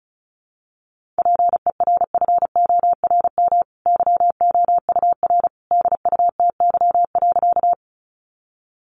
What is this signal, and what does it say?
Morse code, heard as a keyed tone, about 34 wpm, 700 Hz. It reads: PERFORM YOUR DUTY.
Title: Sentences from Top 1500 Words 35wpm